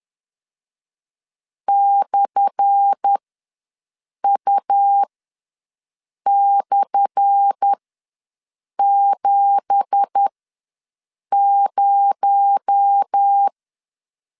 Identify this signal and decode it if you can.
Morse code, heard as a keyed tone, about 10 wpm, spaced as for 6 wpm, 800 Hz. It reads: /U/70
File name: LEZIONI / 10 SIGNS / 10A SLASH ws 10wpm 4121